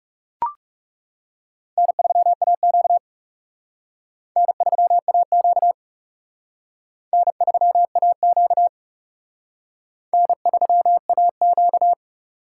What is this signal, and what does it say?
Morse code, heard as a keyed tone, about 40 wpm, 700 Hz. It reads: N3AQ N3AQ N3AQ N3AQ